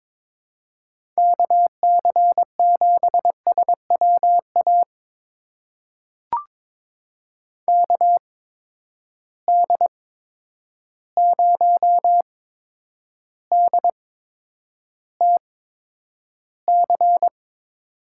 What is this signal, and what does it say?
Morse code, heard as a keyed tone, about 22 wpm, spaced as for 9 wpm, 700 Hz. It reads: KC7SWA KD0DTC
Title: USA Call Signs - 6 Characters - 8x Character Spacing - 22wpm